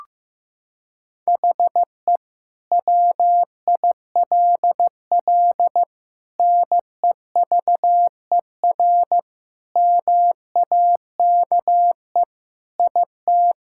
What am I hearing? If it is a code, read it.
Morse code, heard as a keyed tone, about 15 wpm, 700 Hz. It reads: HE WILL NEVER MAKE IT
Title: Sentences from Top 1600 Words 15wpm